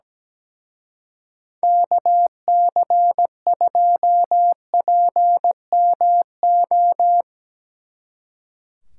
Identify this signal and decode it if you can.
Morse code, heard as a keyed tone, about 18 wpm, 700 Hz. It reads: KC2PMO
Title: Speed Racing - USA 2x3 Call Signs 17wpm